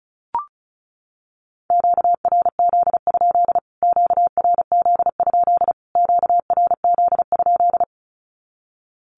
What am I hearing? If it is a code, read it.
Morse code, heard as a keyed tone, about 34 wpm, 700 Hz. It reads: QRZ? QRZ? QRZ?
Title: Q-Codes Repeated 3x 35wpm